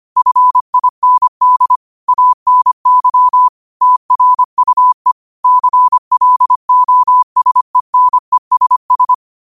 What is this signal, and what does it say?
Morse code, heard as a keyed tone, about 25 wpm, 1000 Hz. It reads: FIND ANY TRUE CLOSENESS